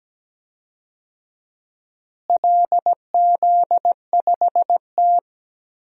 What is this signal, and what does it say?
Morse code, heard as a keyed tone, about 17 wpm, 700 Hz. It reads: LZ5T
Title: DXCC-100 Call Signs - 4 Characters - Repeated 2x - 17wpm